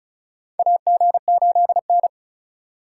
Morse code, heard as a keyed tone, about 34 wpm, 700 Hz. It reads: AG8N
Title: USA Call Signs - 4 Characters - 2x Character Spacing - 35wpm